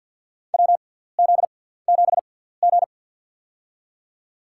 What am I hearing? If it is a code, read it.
Morse code, heard as a keyed tone, about 52 wpm, spaced as for 17 wpm, 700 Hz. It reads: WZ7G
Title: USA Call Signs - 4 Characters - 6x Character Spacing - 50wpm